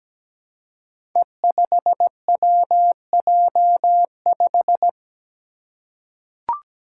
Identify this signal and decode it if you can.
Morse code, heard as a keyed tone, about 17 wpm, 700 Hz. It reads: E5WJ5
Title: Five Challenging Letter-Number Combinations 17wpm